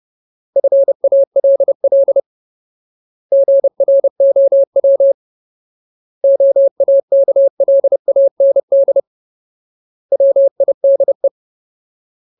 Morse code, heard as a keyed tone, about 30 wpm, 550 Hz. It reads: FALL GROW OAKLAND WIDE
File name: Ultimate Word List / 30WPM Ultimate